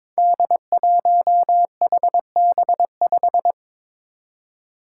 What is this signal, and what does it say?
Morse code, heard as a keyed tone, about 22 wpm, 700 Hz. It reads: D1HB5
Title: Five Challenging Letter-Number Combinations 22wpm